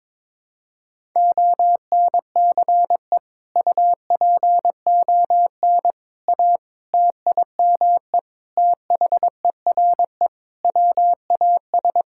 Morse code, heard as a keyed tone, about 22 wpm, 700 Hz. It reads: ONCE UPON A TIME THERE WAS
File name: practice / 22wpm / Sets of 7 Words - 22wpm